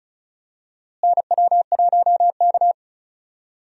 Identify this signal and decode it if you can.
Morse code, heard as a keyed tone, about 36 wpm, 700 Hz. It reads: NW1K
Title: USA Call Signs - 4 Characters - Repeated 3x - 35wpm